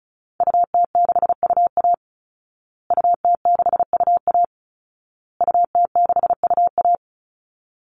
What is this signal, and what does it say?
Morse code, heard as a keyed tone, about 34 wpm, 700 Hz. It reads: UT6UA UT6UA UT6UA
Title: Call Signs Repeated 2x - 35wpm